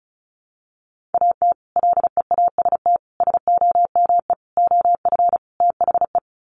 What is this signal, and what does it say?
Morse code, heard as a keyed tone, about 34 wpm, 700 Hz. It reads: AT LEAST SOME OF THE
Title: Sets of 5 Words Encore 35wpm